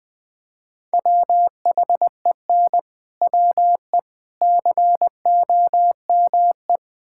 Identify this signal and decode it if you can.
Morse code, heard as a keyed tone, about 20 wpm, 690 Hz. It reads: WHEN WE COME
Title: Build A Phrase - 4-Word Phrases - 20wpm